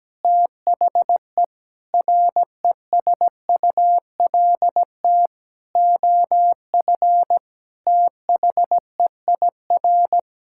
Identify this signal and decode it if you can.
Morse code, heard as a keyed tone, about 17 wpm, 700 Hz. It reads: THE RESULT OF THEIR
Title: Sentences from Top 1000 Words - Review Part 5 - 17wpm